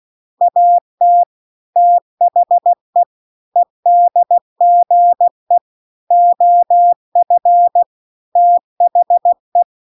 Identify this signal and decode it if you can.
Morse code, heard as a keyed tone, about 16 wpm, 700 Hz. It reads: AT THE EDGE OF THE